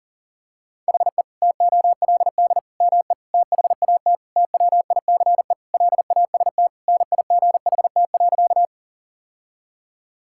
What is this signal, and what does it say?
Morse code, heard as a keyed tone, about 37 wpm, 700 Hz. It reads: HE TOLD ME THAT TWICE LAST NIGHT.
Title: Sentences from Top 1600 Words 40wpm